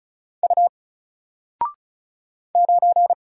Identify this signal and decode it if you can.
Morse code, heard as a keyed tone, about 34 wpm, 700 Hz. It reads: U 9
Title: Single Letter-Number Alt 35wpm